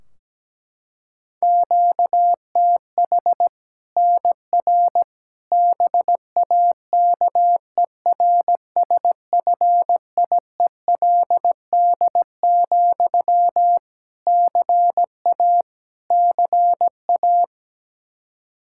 Morse code, heard as a keyed tone, about 17 wpm, 700 Hz. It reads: QTH NR BAKERSFIELD, CA CA
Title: Practice QSO - Exchange 1 - Beginning - 17wpm